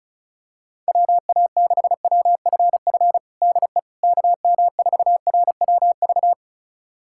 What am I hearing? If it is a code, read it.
Morse code, heard as a keyed tone, about 36 wpm, 700 Hz. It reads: WA6WFF DE KM4RWV
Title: USA Callsign DE Callsign 35wpm